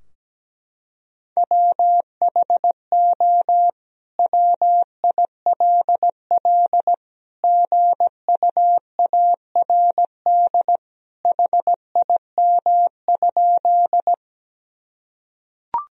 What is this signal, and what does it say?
Morse code, heard as a keyed tone, about 17 wpm, 700 Hz. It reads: WHO WILL GUARD HIM?
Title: Sentences from Top 1000 Words - Review Part 4 - 17wpm